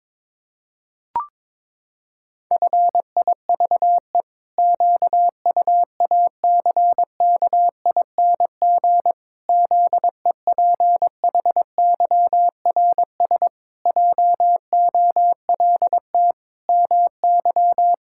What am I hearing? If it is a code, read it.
Morse code, heard as a keyed tone, about 22 wpm, 700 Hz. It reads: FIVE QUACKING ZEPHYRS JOLT MY
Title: Pangram Challenge v2 - 22wpm